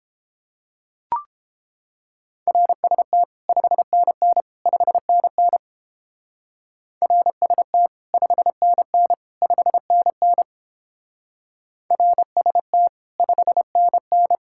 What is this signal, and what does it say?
Morse code, heard as a keyed tone, about 31 wpm, 700 Hz. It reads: RST 5NN 5NN RST 5NN 5NN RST 5NN